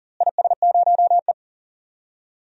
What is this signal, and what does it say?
Morse code, heard as a keyed tone, about 40 wpm, 700 Hz. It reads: IS0E
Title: DXCC-100 Call Signs - 4 Characters - Repeated 2x - 40wpm